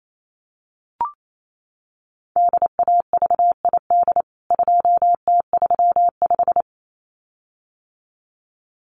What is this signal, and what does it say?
Morse code, heard as a keyed tone, about 28 wpm, 700 Hz. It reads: DAVID 2T35
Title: CWT Exchange 28wpm